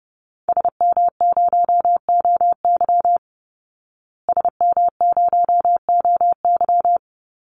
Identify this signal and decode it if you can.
Morse code, heard as a keyed tone, about 30 wpm, 700 Hz. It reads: SM0OY SM0OY
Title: Call Signs Repeated 3x - 30wpm